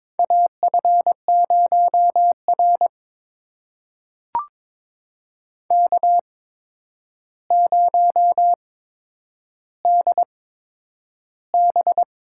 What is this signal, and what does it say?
Morse code, heard as a keyed tone, about 22 wpm, spaced as for 9 wpm, 700 Hz. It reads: AF0R K0DB
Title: USA Call Signs - 4 Characters - 8x Character Spacing - 22wpm